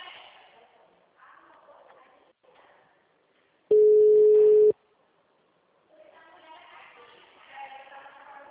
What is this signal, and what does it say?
Ringback tone (Europe, 425 Hz).